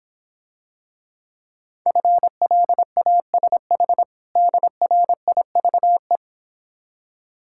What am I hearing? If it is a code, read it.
Morse code, heard as a keyed tone, about 26 wpm, 700 Hz. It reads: FLASH DRIVE